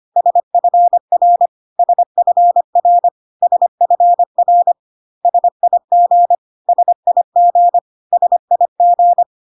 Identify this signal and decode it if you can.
Morse code, heard as a keyed tone, about 25 wpm, 700 Hz. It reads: SFR SFR SFR SIG SIG SIG
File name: CW abbreviations alphabetical repeated 3 times 25 25wpm